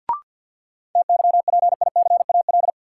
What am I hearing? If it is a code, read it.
Morse code, heard as a keyed tone, about 53 wpm, 700 Hz. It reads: TYPICAL
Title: Sentences from Top 1100 Words - New Word Drill - 50wpm